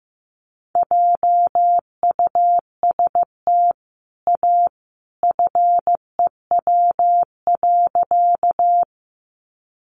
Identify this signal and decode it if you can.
Morse code, heard as a keyed tone, about 15 wpm, 700 Hz. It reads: JUST A FEW.
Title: Sentences from Top 300 Words 15wpm